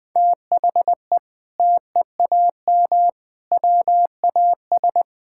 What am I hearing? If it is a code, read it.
Morse code, heard as a keyed tone, about 21 wpm, 700 Hz. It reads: THE TEAM WAS
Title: Sentences with 6-Word Phrases - Spoken Beginning - 20wpm